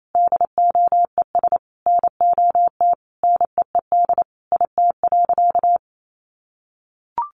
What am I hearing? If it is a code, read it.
Morse code, heard as a keyed tone, about 28 wpm, 700 Hz. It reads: DOES NOT NEED IT.
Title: Sentences from Top 1400 Words 28wpm